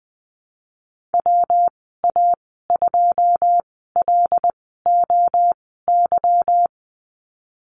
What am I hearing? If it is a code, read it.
Morse code, heard as a keyed tone, about 20 wpm, spaced as for 15 wpm, 700 Hz. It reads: WA2LOY
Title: 154 Lesson 25g - Letter p - Call Signs 20wpm